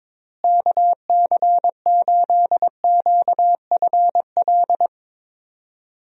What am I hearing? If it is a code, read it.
Morse code, heard as a keyed tone, about 21 wpm, 700 Hz. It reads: KC8QFL
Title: USA Call Signs - 6 Characters - Repeated 3x - 22wpm